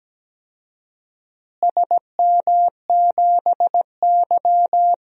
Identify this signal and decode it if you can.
Morse code, heard as a keyed tone, about 17 wpm, 700 Hz. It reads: SM7Y
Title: DXCC-100 Call Signs - 4 Characters - Repeated 3x - 17wpm